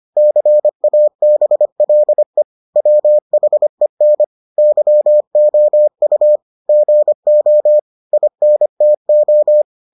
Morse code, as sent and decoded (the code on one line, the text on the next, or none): -.-. .- -... .-.. / .-- .... -. / -.-- --- ..- / --. --- / .. -. - ---
CABLE WHEN YOU GO INTO